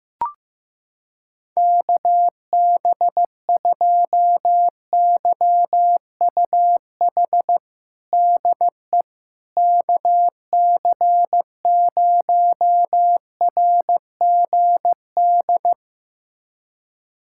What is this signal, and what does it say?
Morse code, heard as a keyed tone, about 15 wpm, 700 Hz. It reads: KB2YUH DE KC0RGD
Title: USA Callsign DE Callsign - 6 Characters - 15wpm